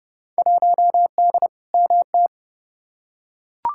Morse code, heard as a keyed tone, about 31 wpm, 700 Hz. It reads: 1D MT
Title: Field Day Exchanges - 30wpm